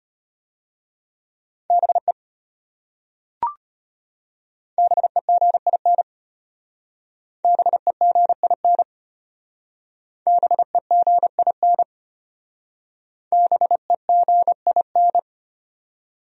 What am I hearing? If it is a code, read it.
Morse code, heard as a keyed tone, about 31 wpm, 700 Hz. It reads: BE BEGIN BEGIN BEGIN BEGIN
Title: Speed-Racing Top 500 Words 25wpm